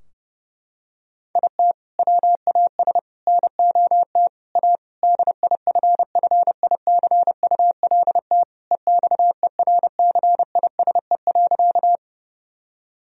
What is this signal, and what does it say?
Morse code, heard as a keyed tone, about 30 wpm, 700 Hz. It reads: IT WAS NOT A DIFFICULT EXERCISE.
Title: Sentences from Top 1000 Words - Review Part 4 - 30wpm